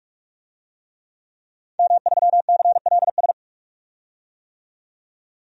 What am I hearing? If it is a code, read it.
Morse code, heard as a keyed tone, about 46 wpm, 700 Hz. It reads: M3CLS
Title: DXCC-100 Call Signs - 5 Characters - 45wpm